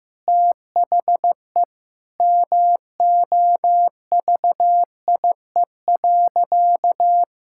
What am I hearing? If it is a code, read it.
Morse code, heard as a keyed tone, about 15 wpm, 700 Hz. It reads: THE MOVIE.
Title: Sentences from Top 1000 Words - Review Part 5 - 15wpm